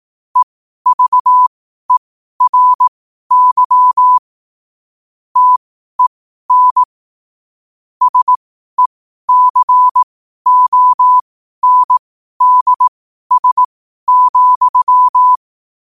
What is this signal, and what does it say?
Morse code, heard as a keyed tone, about 18 wpm, spaced as for 12 wpm, 1000 Hz. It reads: EVERY TEN SECONDS,
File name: qotdfiles / 12wpm